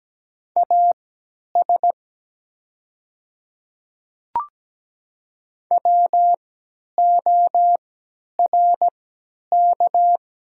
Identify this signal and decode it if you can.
Morse code, heard as a keyed tone, about 17 wpm, spaced as for 10 wpm, 700 Hz. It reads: AS WORK
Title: 3x Character Spacing - Top 200 Words - 17wpm